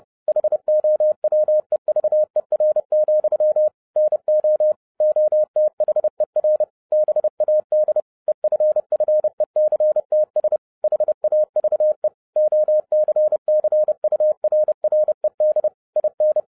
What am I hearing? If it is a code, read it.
Morse code, heard as a keyed tone, about 30 wpm, 610 Hz. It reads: HOWEVER, NO OTHER BAD EFFECTS HAVE OCCURRED IN
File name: 2011-07-mp3 / n60 BACK FROM A BIG ACCIDENT 30wpm